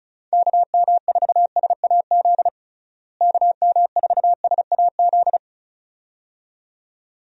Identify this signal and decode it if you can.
Morse code, heard as a keyed tone, about 36 wpm, 700 Hz. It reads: KM4SAZ KM4SAZ
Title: USA Call Signs - 6 Characters - Repeated 2x - 35wpm